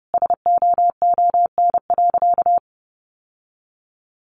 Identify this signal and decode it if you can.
Morse code, heard as a keyed tone, about 30 wpm, 700 Hz. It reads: SOON.
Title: Sentences from Top 1100 Words 30wpm